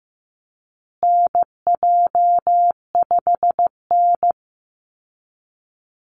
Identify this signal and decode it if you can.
Morse code, heard as a keyed tone, about 15 wpm, 700 Hz. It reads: NJ5N
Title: Call Signs Repeated 3x - 15wpm